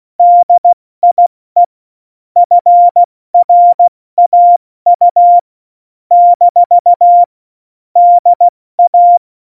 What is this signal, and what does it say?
Morse code, heard as a keyed tone, about 16 wpm, 700 Hz. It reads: DIE FRAU - DA
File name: Morsekurs / Witze / Witz 27 Juni 17, Tempo 70 F80